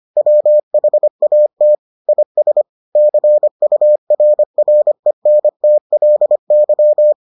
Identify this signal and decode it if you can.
Morse code, heard as a keyed tone, about 25 wpm, 600 Hz. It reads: WHAT IS CURRENTLY